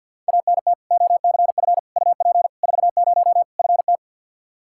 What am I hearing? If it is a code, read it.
Morse code, heard as a keyed tone, about 50 wpm, 700 Hz. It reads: ANT OCF UP 40 FT
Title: K1USN SST QSO Practice 50wpm